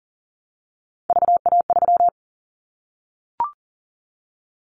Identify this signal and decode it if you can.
Morse code, heard as a keyed tone, about 40 wpm, 700 Hz. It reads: VA3
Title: Callsign Prefixes for Top 100 DXCC - 40wpm